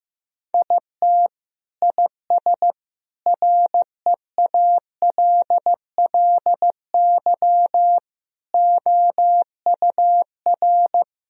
Morse code, heard as a keyed tone, about 16 wpm, 700 Hz. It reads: IT IS REALLY OUR
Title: Sentences from Top 1000 Words - Review Part 2 - 15wpm